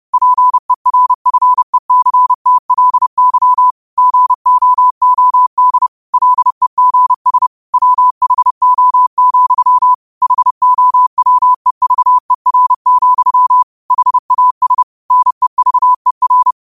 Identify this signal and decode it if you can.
Morse code, heard as a keyed tone, about 30 wpm, 1000 Hz. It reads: PERFECTLY GOOD LEGS WHO, HOWEVER, HAS NEVER